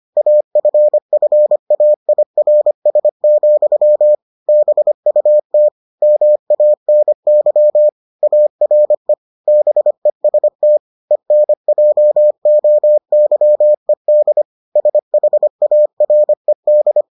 Morse code, heard as a keyed tone, about 25 wpm, 600 Hz. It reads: AFFAIRS, BUT MANY ARE BEST ENJOYED SHARED